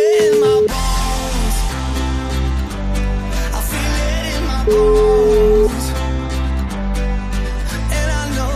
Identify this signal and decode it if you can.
Ringback tone (Europe, 425 Hz), with noise.